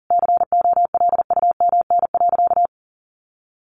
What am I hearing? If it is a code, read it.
Morse code, heard as a keyed tone, about 40 wpm, 700 Hz. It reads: COLUMN.
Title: A Sound Of Thunder 40wpm